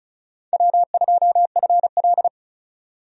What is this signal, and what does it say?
Morse code, heard as a keyed tone, about 36 wpm, 700 Hz. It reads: W2FL
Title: USA Call Signs - 4 Characters - 35wpm